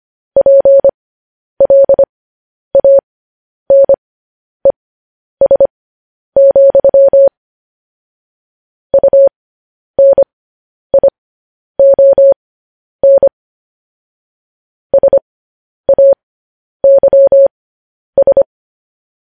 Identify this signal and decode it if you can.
Morse code, heard as a keyed tone, about 25 wpm, spaced as for 10 wpm, 550 Hz. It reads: PLANES, UNION SAYS
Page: News Headlines in Morse Code at 10 WPM